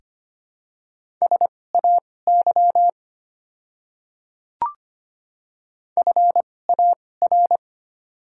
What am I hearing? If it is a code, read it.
Morse code, heard as a keyed tone, about 25 wpm, spaced as for 18 wpm, 700 Hz. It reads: SAY FAR
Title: 2x Character Spacing - Top 300 Words - 25wpm